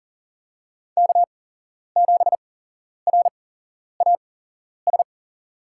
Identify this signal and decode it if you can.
Morse code, heard as a keyed tone, about 40 wpm, spaced as for 11 wpm, 700 Hz. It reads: K7RAS
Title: USA Call Signs - 8x Character Spacing - 40wpm